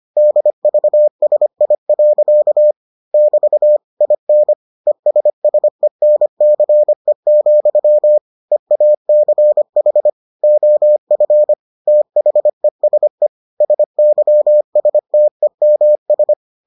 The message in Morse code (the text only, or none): DVSI. <BT> IN ESSENCE, EACH OF THESE SYSTEMS